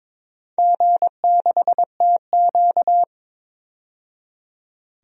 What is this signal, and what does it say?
Morse code, heard as a keyed tone, about 22 wpm, 700 Hz. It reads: G6TQ
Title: DXCC-100 Call Signs - 4 Characters - 22wpm